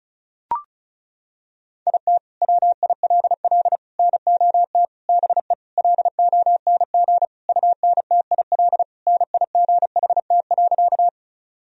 Morse code, heard as a keyed tone, about 35 wpm, 700 Hz. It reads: IT WILL NOT BE LONG UNTIL NIGHT.
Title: Sentences from Top 1000 Words - Review Part 1 - 35wpm